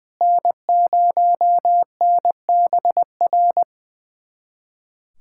Morse code, heard as a keyed tone, about 20 wpm, 700 Hz. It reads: N0NBR
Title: USA Callsign DE Callsign - 5 Characters - 20wpm